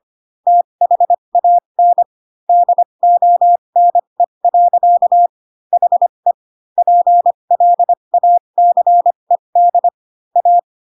Morse code, heard as a keyed tone, about 25 wpm, spaced as for 22 wpm, 700 Hz. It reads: THAN DONE. HE PLACED A